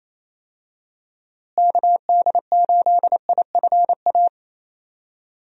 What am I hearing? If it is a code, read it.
Morse code, heard as a keyed tone, about 28 wpm, 700 Hz. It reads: KD8IFA
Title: USA Call Signs - 6 Characters - Repeated 3x - 28wpm